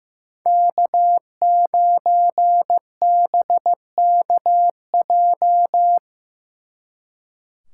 Morse code, heard as a keyed tone, about 18 wpm, 700 Hz. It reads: K9BKJ
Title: Speed Racing - USA 1x3 Call Signs 15wpm